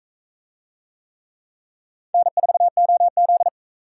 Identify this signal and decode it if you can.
Morse code, heard as a keyed tone, about 42 wpm, 700 Hz. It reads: N4OZ